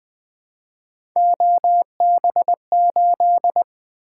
Morse code, heard as a keyed tone, about 20 wpm, 700 Hz. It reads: OB8
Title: Three Letter-Number Combinations - 20wpm